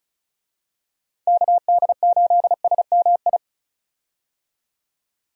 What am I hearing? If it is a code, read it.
Morse code, heard as a keyed tone, about 36 wpm, 700 Hz. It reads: KD8SMI
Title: USA Call Signs - 6 Characters - 35wpm